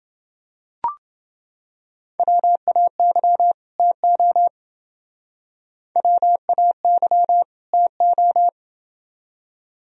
Morse code, heard as a keyed tone, about 29 wpm, 700 Hz. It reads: WAY TO WAY TO